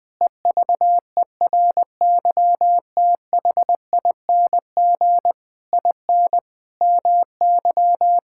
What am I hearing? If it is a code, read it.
Morse code, heard as a keyed tone, about 21 wpm, 700 Hz. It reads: EVERYTHING IN MY
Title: Sets of 8 Words - 20wpm